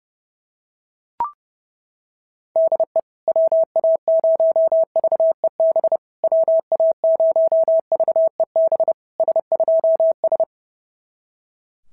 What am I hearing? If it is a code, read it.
Morse code, heard as a keyed tone, about 30 wpm, 650 Hz. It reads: DE WA0VEB WA0VEB S2S
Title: SOTA QSO Practice 30wpm